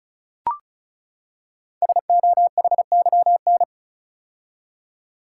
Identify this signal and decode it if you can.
Morse code, heard as a keyed tone, about 36 wpm, 700 Hz. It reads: SOHYN
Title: Code Groups - 5 Letters - 35wpm